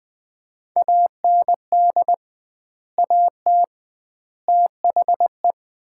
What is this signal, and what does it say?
Morse code, heard as a keyed tone, about 21 wpm, 700 Hz. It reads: AND AT THE
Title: 2x Word-Spacing - Sets of 5 Words - 20wpm